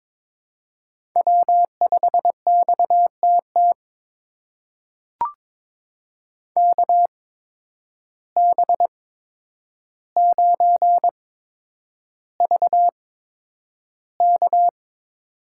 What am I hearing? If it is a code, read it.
Morse code, heard as a keyed tone, about 22 wpm, spaced as for 9 wpm, 700 Hz. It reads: W5XTT KB9VK
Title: USA Call Signs - 5 Characters - 8x Character Spacing - 22wpm